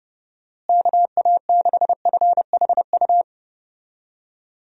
Morse code, heard as a keyed tone, about 31 wpm, 700 Hz. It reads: KA6FHU
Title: USA Call Signs - 6 Characters - Repeated 3x - 30wpm